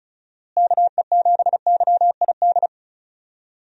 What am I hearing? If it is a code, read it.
Morse code, heard as a keyed tone, about 36 wpm, 700 Hz. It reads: KE7YID
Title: USA Call Signs - 6 Characters - Repeated 3x - 35wpm